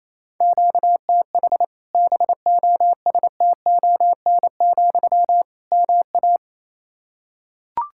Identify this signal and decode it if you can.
Morse code, heard as a keyed tone, about 28 wpm, 700 Hz. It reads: QTH BOSTON, MA
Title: Practice QSO - Exchange 1 - Beginning - 28wpm